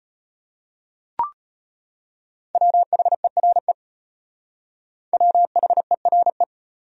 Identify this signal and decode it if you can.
Morse code, heard as a keyed tone, about 31 wpm, 700 Hz. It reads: WHERE WHERE